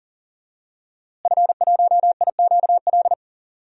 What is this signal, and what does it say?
Morse code, heard as a keyed tone, about 40 wpm, 700 Hz. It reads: F1IQL
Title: DXCC-100 Call Signs - 5 Characters - Repeated 2x - 40wpm